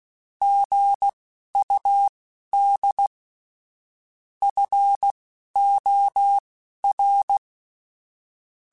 Morse code, heard as a keyed tone, about 16 wpm, spaced as for 11 wpm, 800 Hz. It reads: GUD FOR
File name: QSOs / QSOs-mp3s / WA7QOX 1@10